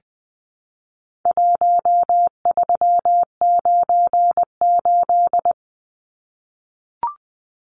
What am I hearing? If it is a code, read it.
Morse code, heard as a keyed tone, about 20 wpm, 700 Hz. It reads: 1398
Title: Numbers - Contest Style - 4-Digit 20wpm